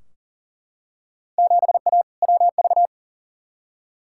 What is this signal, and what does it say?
Morse code, heard as a keyed tone, about 40 wpm, 700 Hz. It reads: 7A WV
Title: Field Day Exchanges Encore - 40wpm